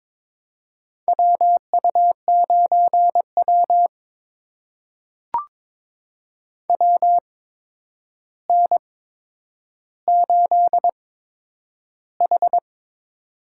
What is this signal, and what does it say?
Morse code, heard as a keyed tone, about 22 wpm, spaced as for 9 wpm, 700 Hz. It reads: WU9W WN8H
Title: USA Call Signs - 4 Characters - 8x Character Spacing - 22wpm